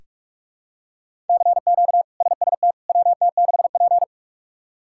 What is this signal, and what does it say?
Morse code, heard as a keyed tone, about 46 wpm, 700 Hz. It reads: CQ SST WT6P
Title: K1USN SST QSO Practice - 45wpm